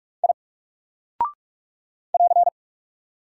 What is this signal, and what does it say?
Morse code, heard as a keyed tone, about 44 wpm, 700 Hz. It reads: I <AR>
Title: Single Character (Letter:Number:Punctuation:Prosign) Alt 45wpm